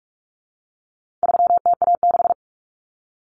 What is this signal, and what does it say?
Morse code, heard as a keyed tone, about 44 wpm, 700 Hz. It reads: 3TA6
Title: Common Cut Numbers - 4-Digit 45wpm